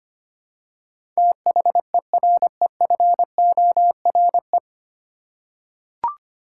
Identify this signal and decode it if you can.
Morse code, heard as a keyed tone, about 25 wpm, 700 Hz. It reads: THEREFORE
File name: Nine Letter Words - 25wpm